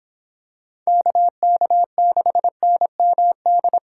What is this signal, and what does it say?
Morse code, heard as a keyed tone, about 26 wpm, 700 Hz. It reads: KK6NMD
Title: Speed Racing - USA Call Signs 22wpm